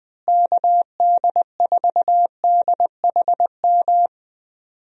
Morse code, heard as a keyed tone, about 20 wpm, 700 Hz. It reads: KD4DHM
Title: USA Call Signs - 6 Characters - Repeated 2x - 20wpm